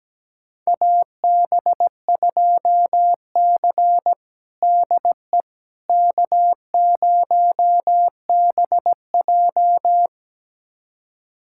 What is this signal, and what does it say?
Morse code, heard as a keyed tone, about 17 wpm, 700 Hz. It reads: AB2C DE K0BJ
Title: USA Callsign DE Callsign - 4 Characters - 17wpm